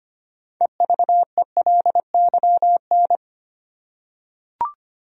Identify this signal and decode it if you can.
Morse code, heard as a keyed tone, about 25 wpm, 700 Hz. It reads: EVELYN
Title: USA Common Names - 25wpm